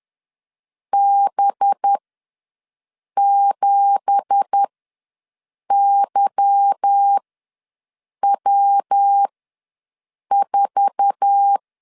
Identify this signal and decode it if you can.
Morse code, heard as a keyed tone, about 10 wpm, spaced as for 6 wpm, 800 Hz. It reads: B7YW4
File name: LEZIONI / 9 2867 ws 10wpm 4215